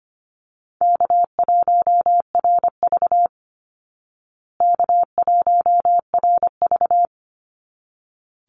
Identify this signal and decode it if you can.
Morse code, heard as a keyed tone, about 25 wpm, 700 Hz. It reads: K1RV K1RV
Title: Call Signs Repeated 3x - 25wpm